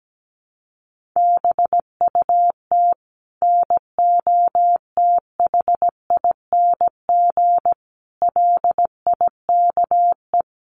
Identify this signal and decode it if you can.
Morse code, heard as a keyed tone, about 17 wpm, 700 Hz. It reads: BUT NOTHING LIKE
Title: Sentences from Top 1100 Words 17wpm